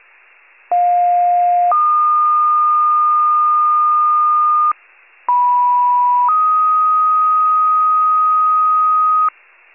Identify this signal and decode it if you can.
Two-tone page, heard as a one-second tone, then a three-second tone, about 710 then 1220 Hz, about 980 then 1250 Hz. It is heard from 0.7 to 9.3 s.